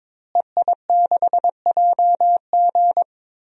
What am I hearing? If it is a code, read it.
Morse code, heard as a keyed tone, about 22 wpm, 700 Hz. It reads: EI6JG
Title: DXCC-100 Call Signs - 5 Characters - Repeated 3x - 22wpm